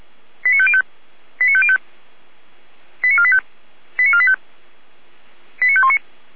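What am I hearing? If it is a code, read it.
Five-tone selective call (ZVEI), heard as quick runs of five tones, tones of about 70 ms. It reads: 89505 89505 89485 89485 89519
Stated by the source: ZVEI five-tone selective call received by SDR.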